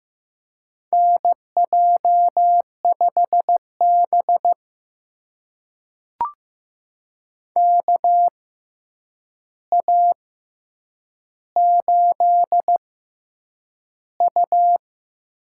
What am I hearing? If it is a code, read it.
Morse code, heard as a keyed tone, about 15 wpm, spaced as for 8 wpm, 700 Hz. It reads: NJ5B KA8U
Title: USA Call Signs - 6x Character Spacing - 15wpm